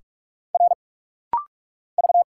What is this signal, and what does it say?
Morse code, heard as a keyed tone, about 46 wpm, 700 Hz. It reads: R V
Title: Single Letters - Rapid-Fire 45wpm